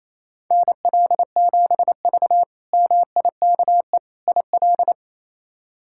Morse code, heard as a keyed tone, about 28 wpm, 700 Hz. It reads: NL7V MIKE IL
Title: K1USN SST QSO Practice - 28wpm